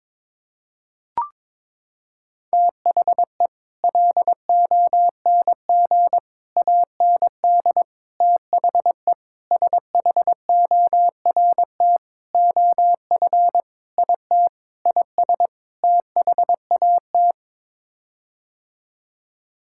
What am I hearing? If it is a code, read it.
Morse code, heard as a keyed tone, about 22 wpm, 700 Hz. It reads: THE LONG AND THE SHORT OF IT IS THAT
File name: Sets of 9 Words - 22wpm